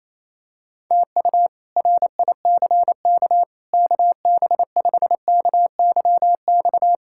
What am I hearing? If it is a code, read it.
Morse code, heard as a keyed tone, about 28 wpm, 700 Hz. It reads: TU RICK KB5KYX
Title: K1USN SST QSO Practice - 28wpm